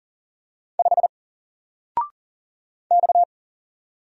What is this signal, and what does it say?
Morse code, heard as a keyed tone, about 40 wpm, 700 Hz. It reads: 5 X
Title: Single Character (Letter:Number:Punctuation:Prosign) Alt 40wpm